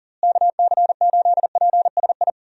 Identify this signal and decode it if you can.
Morse code, heard as a keyed tone, about 40 wpm, 700 Hz. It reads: KC8PSI
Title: USA Call Signs - 6 Characters - Repeated 3x - 40wpm